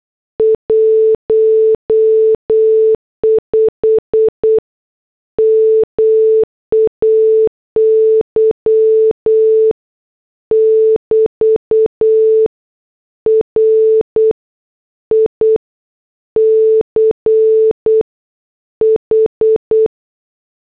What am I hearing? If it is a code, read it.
Morse code, heard as a keyed tone, about 8 wpm, 440 Hz. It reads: 15 MAY <BT> RICH